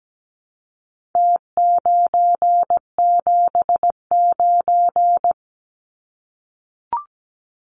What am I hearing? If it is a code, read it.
Morse code, heard as a keyed tone, about 17 wpm, 700 Hz. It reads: T979
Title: Numbers - Contest Style - 4-Digit 17wpm